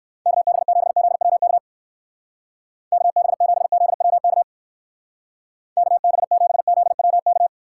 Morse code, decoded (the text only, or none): KB7ZWK KB7ZWK KB7ZWK